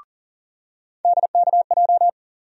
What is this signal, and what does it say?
Morse code, heard as a keyed tone, about 40 wpm, 700 Hz. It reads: DKJ
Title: USA Call Sign Postfixes - 3 Letters 40wpm